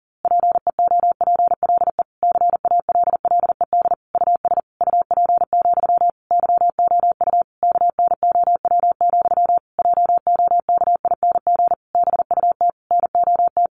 Morse code, heard as a keyed tone, about 40 wpm, 700 Hz. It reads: PEOPLE CALLED US UP, YOU KNOW, JOKING BUT NOT
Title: A Sound Of Thunder 40wpm